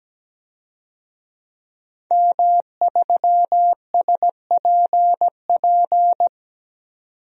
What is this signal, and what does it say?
Morse code, heard as a keyed tone, about 17 wpm, 700 Hz. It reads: M3SPP
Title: DXCC-100 Call Signs - 5 Characters - 17wpm